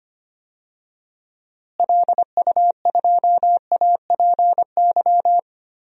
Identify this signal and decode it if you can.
Morse code, heard as a keyed tone, about 25 wpm, 700 Hz. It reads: LU2APY
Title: DXCC-100 Call Signs - 6 Characters - Repeated 2x - 25wpm